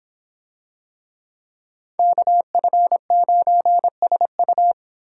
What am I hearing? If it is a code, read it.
Morse code, heard as a keyed tone, about 26 wpm, 700 Hz. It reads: KF9SU